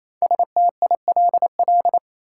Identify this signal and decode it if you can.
Morse code, heard as a keyed tone, about 28 wpm, 700 Hz. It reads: STILL
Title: Sentences from Top 300 Words - New Word Drill - 28wpm